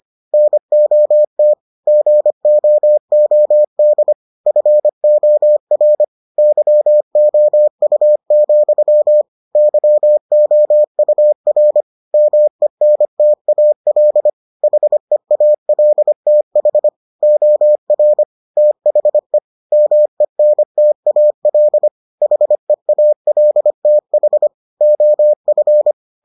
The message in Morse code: NOT GOOD FOR YOU, YOUR MENTAL HEALTH OR THE MENTAL HEALTH OF

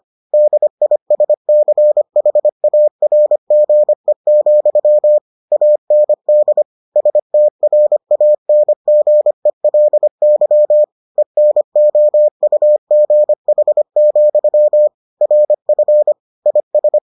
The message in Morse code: DISCHARGE, AND STRANGELY ENOUGH, RF IS

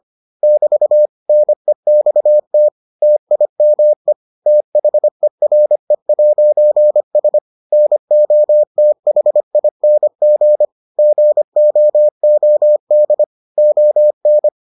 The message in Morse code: <BT> NEXT TIME THERE'S NOTHING GOOD ON

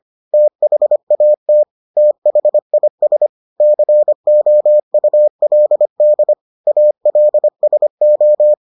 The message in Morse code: THAT THIS COULD ALSO

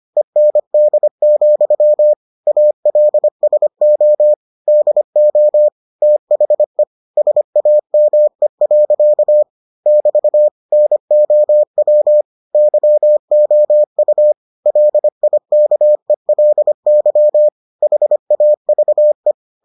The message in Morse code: END, ALSO DO THE SAME. <BT> NOW YOU LIKELY HAVE

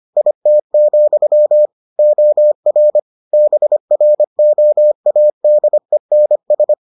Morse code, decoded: IT, OR BROADENS

